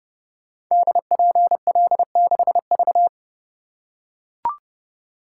Morse code, heard as a keyed tone, about 30 wpm, 700 Hz. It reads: DPL6V
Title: Code Groups - 5 Letter:Numbers - 30wpm